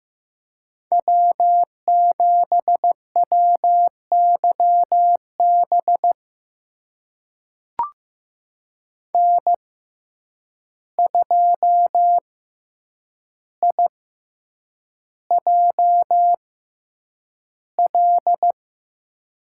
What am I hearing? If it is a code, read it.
Morse code, heard as a keyed tone, about 15 wpm, spaced as for 8 wpm, 700 Hz. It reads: W7WYB N2IJL
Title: USA Call Signs - 5 Characters - 6x Character Spacing - 15wpm